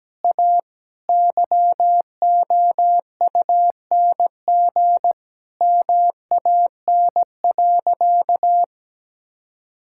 Morse code, heard as a keyed tone, about 17 wpm, 700 Hz. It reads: A YOUNG MAN.
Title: Sentences from Top 1000 Words - Review Part 5 - 17wpm